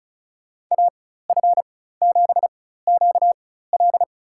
Morse code, heard as a keyed tone, about 36 wpm, spaced as for 16 wpm, 700 Hz. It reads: AF7QL
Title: USA Call Signs - 4x Character Spacing - 35wpm